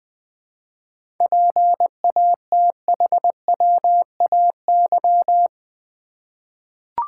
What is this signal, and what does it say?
Morse code, heard as a keyed tone, about 20 wpm, 700 Hz. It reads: PATHWAY
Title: 7 Letter Words - First 3 Letters Given - 20wpm